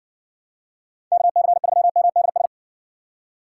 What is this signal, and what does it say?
Morse code, heard as a keyed tone, about 68 wpm, 700 Hz. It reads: KC3MDS